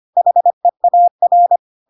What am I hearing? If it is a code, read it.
Morse code, heard as a keyed tone, about 25 wpm, 700 Hz. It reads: HEAR
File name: Pinocchio / Pinocchio ch2 25 25 ik1hge